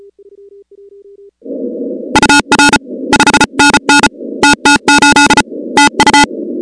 Morse code, heard as a keyed tone, about 34 wpm, 860 Hz, with noise. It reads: UR 5NN TT8 TU
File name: contests / 2019-cw-wpx-kt5j-k5pi- -2330 0-16